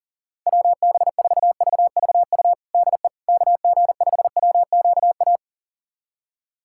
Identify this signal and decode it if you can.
Morse code, heard as a keyed tone, about 40 wpm, 700 Hz. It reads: WB4VVU DE KG5WQA
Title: USA Callsign DE Callsign - 6 Characters - 40wpm